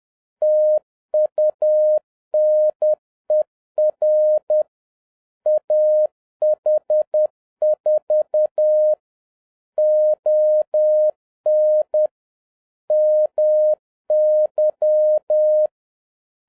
Morse code, heard as a keyed tone, about 10 wpm, 610 Hz. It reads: TUNER AH4 ON MY
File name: 2011-07-mp3 / n41 EXCITING QRP CW OPERATION 10wpm